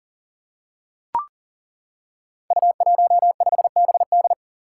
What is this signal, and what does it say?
Morse code, heard as a keyed tone, about 38 wpm, 690 Hz. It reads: U15BD
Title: Five Challenging Letter-Number Combinations 40wpm